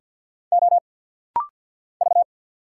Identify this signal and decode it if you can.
Morse code, heard as a keyed tone, about 48 wpm, 700 Hz. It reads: O V
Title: Single Letter-Number - Rapid-Fire 50wpm